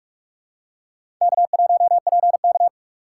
Morse code, heard as a keyed tone, about 45 wpm, 700 Hz. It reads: K1PK
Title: USA Call Signs - 4 Characters - 8x Character Spacing - 45wpm